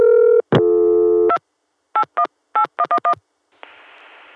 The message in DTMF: 2 818111